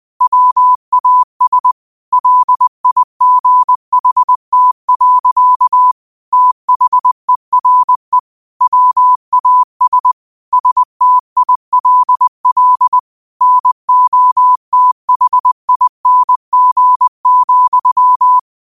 Morse code, heard as a keyed tone, about 20 wpm, 1000 Hz. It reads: WAS LIGHT. THERE WAS STILL NOTHING,